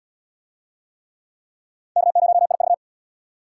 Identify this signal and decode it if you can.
Morse code, heard as a keyed tone, about 64 wpm, 700 Hz. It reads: K0IV